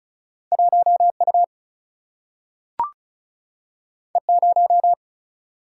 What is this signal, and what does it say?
Morse code, heard as a keyed tone, about 36 wpm, 700 Hz. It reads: 1 U E 0
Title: Single-Letter Single-Number 35wpm